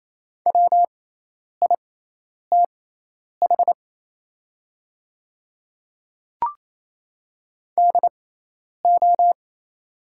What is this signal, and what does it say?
Morse code, heard as a keyed tone, about 28 wpm, spaced as for 10 wpm, 700 Hz. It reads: WITH DO
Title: 6x Character Spacing - Top 100 Words - 28wpm